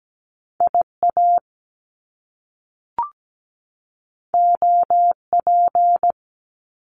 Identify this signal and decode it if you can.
Morse code, heard as a keyed tone, about 17 wpm, 700 Hz. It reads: IA OP
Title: Top Two Letter Combinations in Words 17wpm